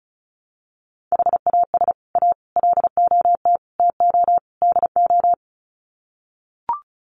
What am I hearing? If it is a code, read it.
Morse code, heard as a keyed tone, about 34 wpm, 700 Hz. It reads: HAS A LOT TO DO
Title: Sets of 5 Words Encore2 35wpm